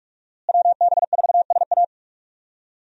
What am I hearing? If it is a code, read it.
Morse code, heard as a keyed tone, about 42 wpm, 700 Hz. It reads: WB4SA
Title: USA Call Signs - 5 Characters - Repeated 2x - 45wpm